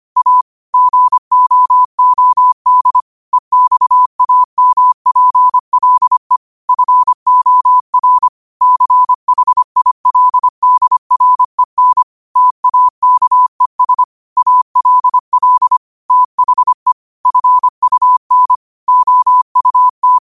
Morse code, heard as a keyed tone, about 25 wpm, 1000 Hz. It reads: A GOOD EXAMPLE FOR CHILDREN TAKES ALL THE FUN OUT
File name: qotdfiles / 25wpm